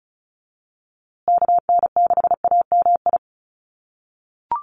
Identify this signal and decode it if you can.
Morse code, heard as a keyed tone, about 34 wpm, 700 Hz. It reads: KN6AMI
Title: USA Call Signs Repeated 3x 35wpm